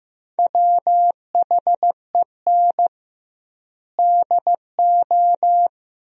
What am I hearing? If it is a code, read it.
Morse code, heard as a keyed tone, about 15 wpm, 700 Hz. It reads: WHEN DO
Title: 2x Word-Spacing - Sentences from Top 400 Words - 15wpm